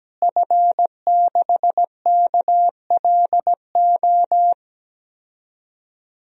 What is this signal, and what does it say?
Morse code, heard as a keyed tone, about 17 wpm, 700 Hz. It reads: F6KLO
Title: DXCC-100 Call Signs - 5 Characters - Repeated 3x - 17wpm